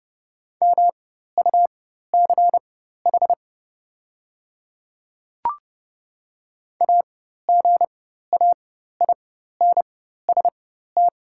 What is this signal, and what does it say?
Morse code, heard as a keyed tone, about 30 wpm, spaced as for 14 wpm, 690 Hz. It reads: MUCH AGAINST
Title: 4x Character Spacing - Top 500 Words - 30wpm